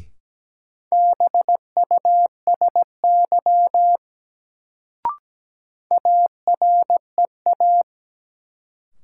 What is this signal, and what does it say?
Morse code, heard as a keyed tone, about 17 wpm, 700 Hz. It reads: BUSY AREA
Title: Sentences from Top 1000 Words - Review Part 2 - New Word Drill - 17wpm